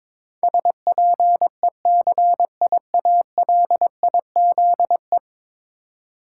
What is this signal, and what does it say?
Morse code, heard as a keyed tone, about 22 wpm, 700 Hz. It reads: SPECIALIZE
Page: morse-code-ninja-practice-22wpm